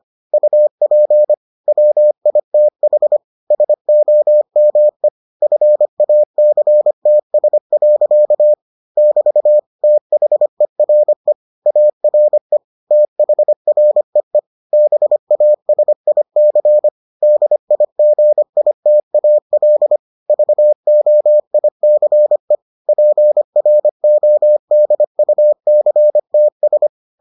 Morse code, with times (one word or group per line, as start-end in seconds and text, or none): 0.3-1.3 UP
1.7-3.2 WITH
3.5-5.1 SOME
5.4-8.5 FACTS.
9.0-9.6 <BT>
9.8-11.3 THERE
11.7-12.6 ARE
12.9-14.4 THREE
14.7-16.9 BASIC
17.2-20.0 DIGITAL
20.3-22.6 VOICE
22.9-26.9 PRODUCTS